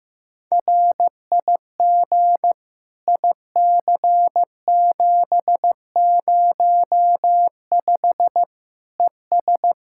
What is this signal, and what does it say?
Morse code, heard as a keyed tone, about 15 wpm, 700 Hz. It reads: RIG IC705 ES
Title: Practice QSO - Exchange 2 - Info - 15wpm